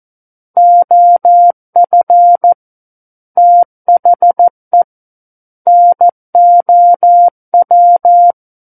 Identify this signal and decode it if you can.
Morse code, heard as a keyed tone, about 14 wpm, 700 Hz. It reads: OF THE NOW
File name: Book One / 05-Chapter 5 - 14 WPM